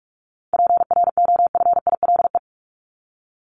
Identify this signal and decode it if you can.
Morse code, heard as a keyed tone, about 44 wpm, 700 Hz. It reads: PROFILE
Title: Top 1501-1750 Words 45wpm